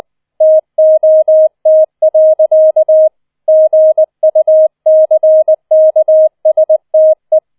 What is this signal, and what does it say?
Morse code, heard as a keyed tone, about 20 wpm, 630 Hz. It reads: TOT. GUCKSTE
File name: 20wpm